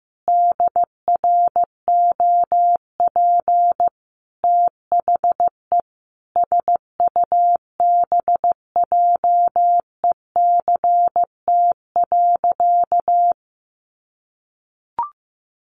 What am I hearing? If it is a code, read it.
Morse code, heard as a keyed tone, about 15 wpm, 700 Hz. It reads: DROP THE SUBJECT.
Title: Sentences from Top 1400 Words 15wpm